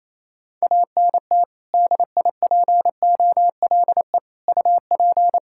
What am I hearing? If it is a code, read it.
Morse code, heard as a keyed tone, about 28 wpm, 700 Hz. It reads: ANT DIPOLE UP
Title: Practice QSO - Exchange 2 - Info - 28wpm